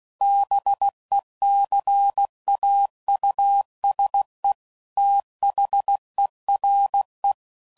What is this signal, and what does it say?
Morse code, heard as a keyed tone, about 16 wpm, 800 Hz. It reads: BECAUSE THERE
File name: CW Code Groups @15WPM / Technical Subjects / Gates Intro